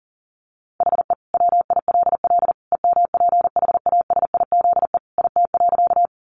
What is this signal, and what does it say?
Morse code, heard as a keyed tone, about 40 wpm, 700 Hz. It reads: HE WILL EMPHASIZE IT.
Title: Sentences from Top 1500 Words 40wpm